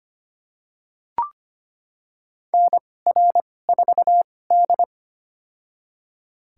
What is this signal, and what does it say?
Morse code, heard as a keyed tone, about 25 wpm, spaced as for 18 wpm, 700 Hz. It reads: NR4D
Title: USA Call Signs - 4 Characters - 2x Character Spacing - 25wpm